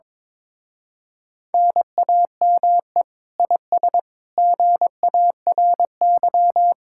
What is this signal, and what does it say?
Morse code, heard as a keyed tone, about 22 wpm, 700 Hz. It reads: NAME IS GARY
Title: Speed-Racing Common QSO Phrases - 22wpm